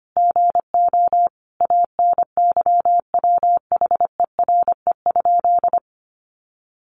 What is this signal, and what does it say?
Morse code, heard as a keyed tone, about 25 wpm, 700 Hz. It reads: GO ANYWHERE?
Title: Sentences from Top 1400 Words 25wpm